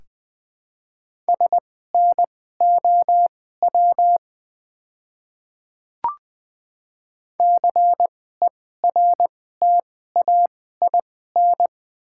Morse code, heard as a keyed tone, about 20 wpm, spaced as for 15 wpm, 700 Hz. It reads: SNOW CERTAIN
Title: 2x Character Spacing - Top 500 Words - 20wpm